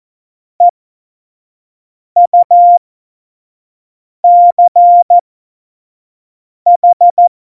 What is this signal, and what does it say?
Morse code, heard as a keyed tone, about 14 wpm, spaced as for 5 wpm, 700 Hz. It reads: EUCH
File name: Morsekurs / Witze / Witz 08. Dezember2020, 25 F70